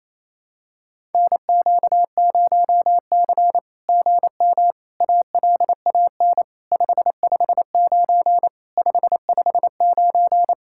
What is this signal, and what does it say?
Morse code, heard as a keyed tone, about 27 wpm, 700 Hz. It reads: NQ0C GM ALAN 559 559
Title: SOTA QSO Practice 28wpm